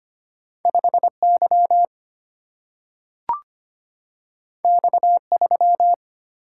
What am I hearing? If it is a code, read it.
Morse code, heard as a keyed tone, about 25 wpm, 700 Hz. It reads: 5Y X3
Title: Single-Letter Single-Number 25wpm